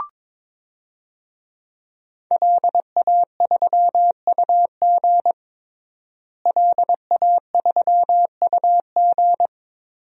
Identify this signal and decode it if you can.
Morse code, heard as a keyed tone, about 22 wpm, 700 Hz. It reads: LA3UG LA3UG
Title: DXCC-100 Call Signs - 5 Characters - Repeated 2x - 22wpm